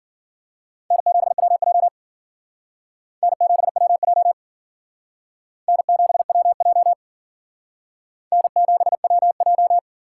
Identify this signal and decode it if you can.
Morse code, heard as a keyed tone, about 51 wpm, 700 Hz. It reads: N7WJ N7WJ N7WJ N7WJ